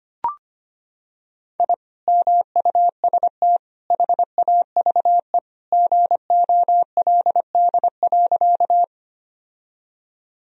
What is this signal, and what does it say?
Morse code, heard as a keyed tone, about 25 wpm, 700 Hz. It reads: I MUST HAVE GOLD.
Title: Sentences from Top 1000 Words - Review Part 3 - 25wpm